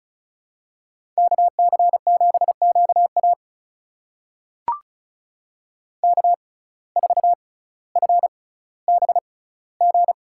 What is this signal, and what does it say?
Morse code, heard as a keyed tone, about 36 wpm, spaced as for 18 wpm, 700 Hz. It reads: KC7QA K4FBG
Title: USA Call Signs - 5 Characters - 6x Character Spacing - 35wpm